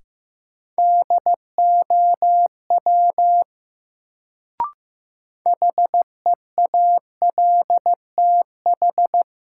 Morse code, heard as a keyed tone, about 15 wpm, 700 Hz. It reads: DOW HEALTH
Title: Sets of 5 Words - New Word Drill - 15wpm